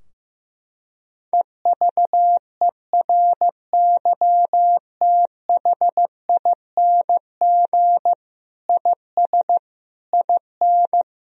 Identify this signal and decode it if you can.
Morse code, heard as a keyed tone, about 15 wpm, 700 Hz. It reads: EVERYTHING IS IN
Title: Sentences from Top 1000 Words - Review Part 2 - 15wpm